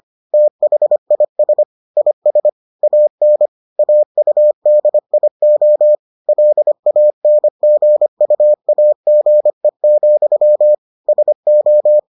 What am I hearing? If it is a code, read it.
Morse code, heard as a keyed tone, about 25 wpm, 600 Hz. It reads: THIS IS AN AUDIO LANGUAGE, SO